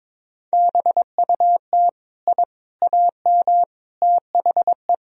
Morse code, heard as a keyed tone, about 22 wpm, 700 Hz. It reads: BUT I AM THE
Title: Sentences from Top 1000 Words - Review Part 4 - 22wpm